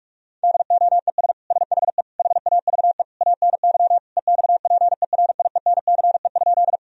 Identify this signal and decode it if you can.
Morse code, heard as a keyed tone, about 46 wpm, 700 Hz. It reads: DOES SHE HAVE ANY EXPERIENCE?
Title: Sentences from Top 1000 Words - Review Part 3 - 45wpm